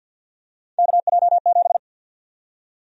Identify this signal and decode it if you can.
Morse code, heard as a keyed tone, about 51 wpm, 700 Hz. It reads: KJ7
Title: USA Call Sign Prefixes - 50wpm